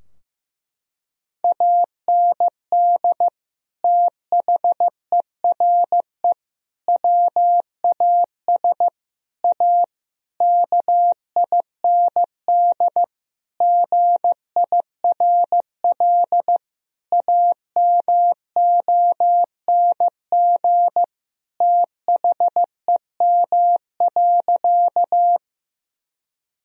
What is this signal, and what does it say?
Morse code, heard as a keyed tone, about 15 wpm, 700 Hz. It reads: AND THERE WAS A KIND GIRL AMONG THEM.
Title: Sentences from Top 1000 Words - Review Part 3 - 15wpm